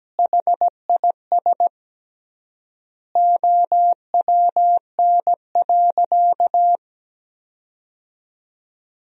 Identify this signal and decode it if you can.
Morse code, heard as a keyed tone, about 17 wpm, 700 Hz. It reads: HIS OWN.
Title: 3x Word-Spacing - Sentences from Top 300 Words - 17wpm